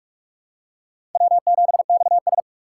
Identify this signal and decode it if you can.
Morse code, heard as a keyed tone, about 46 wpm, 700 Hz. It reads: W7XS
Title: USA Call Signs - 4 Characters - 3x Character Spacing - 45wpm